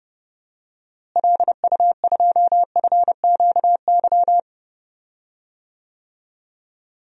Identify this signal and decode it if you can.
Morse code, heard as a keyed tone, about 30 wpm, 700 Hz. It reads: LU2FQY
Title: DXCC-100 Call Signs - 6 Characters - Repeated 2x - 30wpm